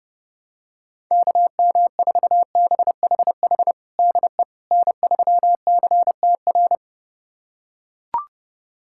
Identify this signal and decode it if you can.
Morse code, heard as a keyed tone, about 31 wpm, 700 Hz. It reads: KM4BHH DE N3CTR
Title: USA Callsign DE Callsign 30wpm